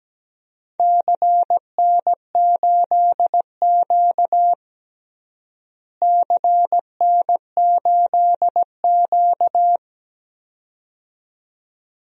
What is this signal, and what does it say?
Morse code, heard as a keyed tone, about 17 wpm, 700 Hz. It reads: CN8Q CN8Q
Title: DXCC-100 Call Signs - 4 Characters - Repeated 2x - 17wpm